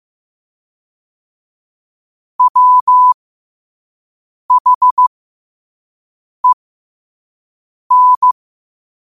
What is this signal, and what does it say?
Morse code, heard as a keyed tone, about 15 wpm, spaced as for 5 wpm, 1000 Hz. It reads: WHEN
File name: qotdfiles / 5wpm